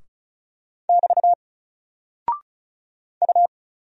Morse code, heard as a keyed tone, about 36 wpm, 700 Hz. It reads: <BT> U
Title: Single Character (Letter:Number:Punctuation:Prosign) Alt 35wpm